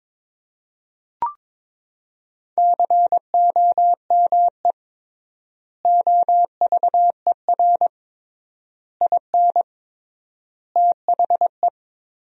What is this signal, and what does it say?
Morse code, heard as a keyed tone, about 21 wpm, 700 Hz. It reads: COME OVER IN THE
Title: 3x Word-Spacing - Sentences from Top 300 Words - 22wpm